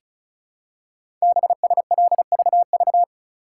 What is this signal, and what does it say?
Morse code, heard as a keyed tone, about 36 wpm, 690 Hz. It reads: BSLVV
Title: Code Groups - 5 Letters - 35wpm